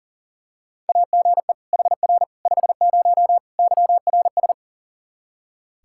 Morse code, heard as a keyed tone, about 40 wpm, 700 Hz. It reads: AGE HR 50 YRS
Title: K1USN SST QSO Practice 40wpm